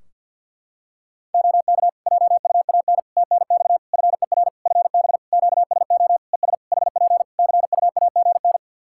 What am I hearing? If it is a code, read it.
Morse code, heard as a keyed tone, about 51 wpm, 700 Hz. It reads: OK JUAN TNX FER FB QSO ES HP CUAGN
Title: Practice QSO - Exchange 4 - Ending - 50wpm